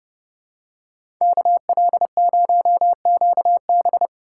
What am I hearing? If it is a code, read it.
Morse code, heard as a keyed tone, about 30 wpm, 700 Hz. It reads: KL0QB
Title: USA Call Signs - 5 Characters - 4x Character Spacing - 30wpm